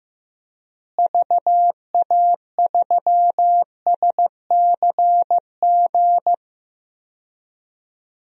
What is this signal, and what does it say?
Morse code, heard as a keyed tone, about 15 wpm, 700 Hz. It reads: VA3SCG
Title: DXCC-100 Call Signs - 6 Characters - Repeated 2x - 15wpm